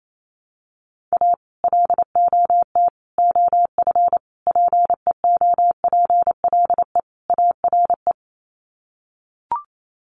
Morse code, heard as a keyed tone, about 28 wpm, 700 Hz. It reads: A LOT OF PEOPLE ARE
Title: Sets of 5 Words 28wpm